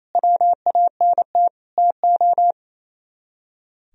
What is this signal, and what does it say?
Morse code, heard as a keyed tone, about 33 wpm, 700 Hz. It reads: WANT TO
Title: Speed-Racing - Sets of 4 Words 28wpm